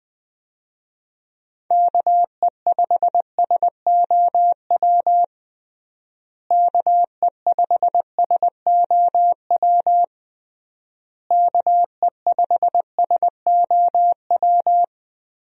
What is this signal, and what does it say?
Morse code, heard as a keyed tone, about 20 wpm, 700 Hz. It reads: KE5SOW KE5SOW KE5SOW
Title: USA Call Signs - 6 Characters - Repeated 3x - 20wpm